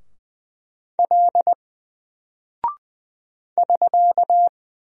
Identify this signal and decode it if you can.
Morse code, heard as a keyed tone, about 20 wpm, 700 Hz. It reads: L <SK>
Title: Single Character (Letter:Number:Punctuation:Prosign) 20wpm